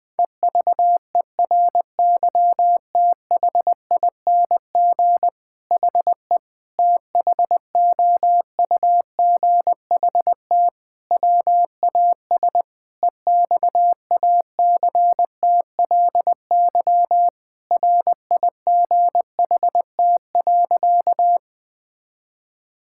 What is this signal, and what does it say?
Morse code, heard as a keyed tone, about 20 wpm, 700 Hz. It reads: EVERYTHING HE THOUGHT WAS EXACTLY RIGHT.
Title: Sentences from Top 1000 Words - Review Part 6 - 20wpm